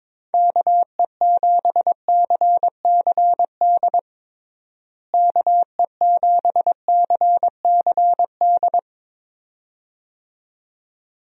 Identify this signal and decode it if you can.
Morse code, heard as a keyed tone, about 22 wpm, 700 Hz. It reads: KE7CCD KE7CCD
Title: USA Call Signs - 6 Characters - Repeated 2x - 22wpm